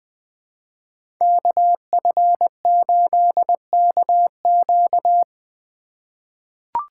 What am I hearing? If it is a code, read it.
Morse code, heard as a keyed tone, about 20 wpm, 700 Hz. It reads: KF8KQ
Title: USA Call Signs - 5 Characters - 2x Character Spacing - 20wpm